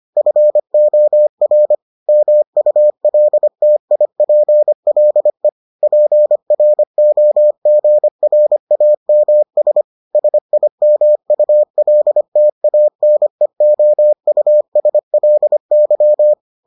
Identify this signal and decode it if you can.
Morse code, heard as a keyed tone, about 25 wpm, 600 Hz. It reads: FOR MULTIPLE PROGRAMS SIMULTANEOUSLY